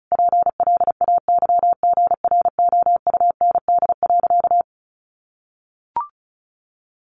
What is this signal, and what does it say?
Morse code, heard as a keyed tone, about 34 wpm, 700 Hz. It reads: PLAYGROUND.
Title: Additional Easy Phrases 35wpm